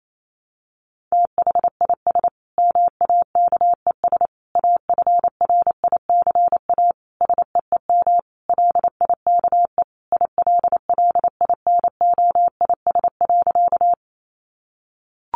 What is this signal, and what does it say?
Morse code, heard as a keyed tone, about 28 wpm, 700 Hz. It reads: THIS MAKES AFRICA SEEM LIKE ILLINOIS.
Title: A Sound Of Thunder 28wpm